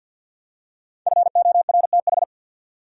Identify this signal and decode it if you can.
Morse code, heard as a keyed tone, about 47 wpm, 700 Hz. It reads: FORTH
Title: Five Letter Words - 50wpm